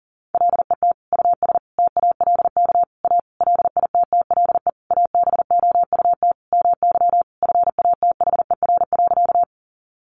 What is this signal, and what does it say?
Morse code, heard as a keyed tone, about 40 wpm, 700 Hz. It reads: LET US TALK A LITTLE ABOUT MY FATHER.
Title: Sentences from Top 400 Words 40wpm